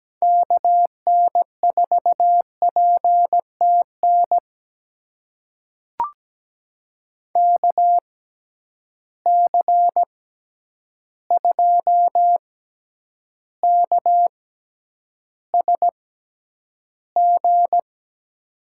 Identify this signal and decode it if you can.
Morse code, heard as a keyed tone, about 17 wpm, spaced as for 9 wpm, 700 Hz. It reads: KN4PTN KC2KSG
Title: USA Call Signs - 6 Characters - 6x Character Spacing - 17wpm